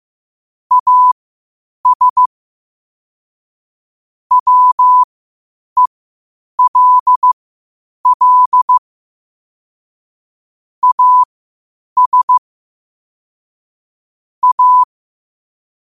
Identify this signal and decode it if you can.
Morse code, heard as a keyed tone, about 15 wpm, spaced as for 8 wpm, 1000 Hz. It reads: AS WELL AS A
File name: qotdfiles / 8wpm